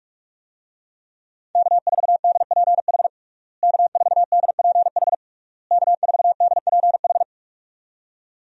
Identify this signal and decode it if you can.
Morse code, heard as a keyed tone, about 41 wpm, 700 Hz. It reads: K4DPH K4DPH K4DPH
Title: USA Call Signs - 5 Characters - Repeated 3x - 45wpm